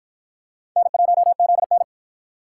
Morse code, heard as a keyed tone, about 53 wpm, 700 Hz. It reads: N1ZN